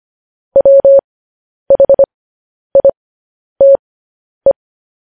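Morse code, heard as a keyed tone, about 25 wpm, spaced as for 10 wpm, 550 Hz. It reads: WHITE
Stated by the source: Morse code news headlines sent at 10 wpm.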